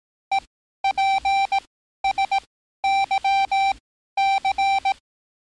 Morse code, heard as a keyed tone, about 18 wpm, spaced as for 12 wpm, 780 Hz. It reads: EPSYC
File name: training / NAVPERS 11049to11060 / NAVPERS 11054 RA RB